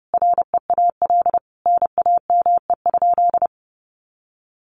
Morse code, heard as a keyed tone, about 30 wpm, 700 Hz. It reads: REAL NAME?
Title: Sentences from Top 300 Words 30wpm